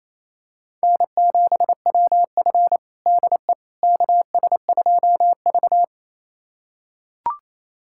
Morse code, heard as a keyed tone, about 28 wpm, 700 Hz. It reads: N7WF DE KS2V
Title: USA Callsign DE Callsign - 4 Characters - 28wpm